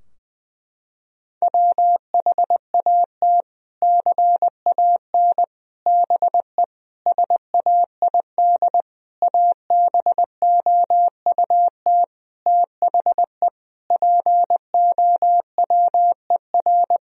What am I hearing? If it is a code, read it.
Morse code, heard as a keyed tone, about 20 wpm, 700 Hz. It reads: WHAT CAN BE SAID ABOUT THE POWER
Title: Sentences from Top 1000 Words - Review Part 4 - 20wpm